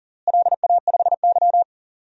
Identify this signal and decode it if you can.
Morse code, heard as a keyed tone, about 40 wpm, 700 Hz. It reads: LA5Y
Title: DXCC-100 Call Signs - 4 Characters - Repeated 2x - 40wpm